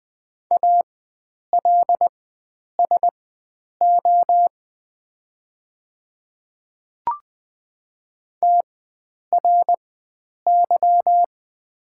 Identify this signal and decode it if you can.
Morse code, heard as a keyed tone, about 20 wpm, spaced as for 9 wpm, 690 Hz. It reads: ALSO TRY
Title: 4x Character Spacing - Top 500 Words - 20wpm